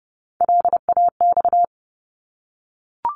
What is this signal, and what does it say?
Morse code, heard as a keyed tone, about 30 wpm, 700 Hz. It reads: LAX
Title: ARRL Sections 30wpm